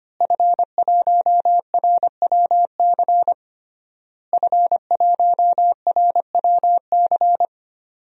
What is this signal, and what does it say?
Morse code, heard as a keyed tone, about 25 wpm, 700 Hz. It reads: F1RWC F1RWC
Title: DXCC-100 Call Signs - 5 Characters - Repeated 3x - 25wpm